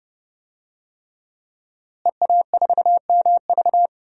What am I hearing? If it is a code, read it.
Morse code, heard as a keyed tone, about 30 wpm, 700 Hz. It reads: EA4MV
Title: DXCC-100 Call Signs - 5 Characters - Repeated 3x - 30wpm